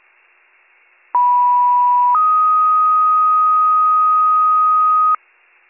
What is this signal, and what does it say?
Two-tone page, heard as a one-second tone, then a three-second tone, about 980 then 1250 Hz. It is heard from 1.1 to 5.2 s.